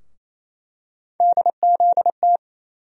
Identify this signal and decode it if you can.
Morse code, heard as a keyed tone, about 28 wpm, 700 Hz. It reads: DZT
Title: USA Call Sign Postfixes - 3 Letters 28wpm